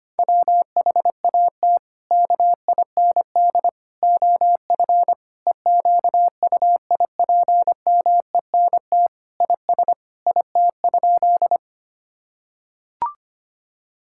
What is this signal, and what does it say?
Morse code, heard as a keyed tone, about 25 wpm, 700 Hz. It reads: WHAT KIND OF EQUIPMENT IS IT?
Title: Sentences from Top 1000 Words - Review Part 5 - 25wpm